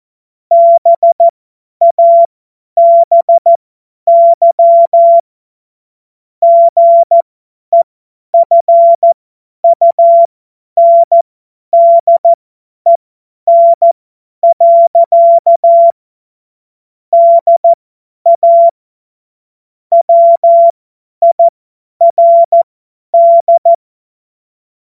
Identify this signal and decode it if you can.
Morse code, heard as a keyed tone, about 14 wpm, spaced as for 10 wpm, 680 Hz. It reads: BABY GEFUNDEN. DA WIRD
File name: Witz-05-August-2025-50-F70